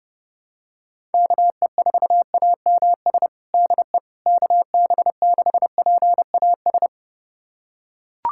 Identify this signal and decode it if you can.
Morse code, heard as a keyed tone, about 30 wpm, 700 Hz. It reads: KE4AMS DE KB6PAS
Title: USA Callsign DE Callsign 30wpm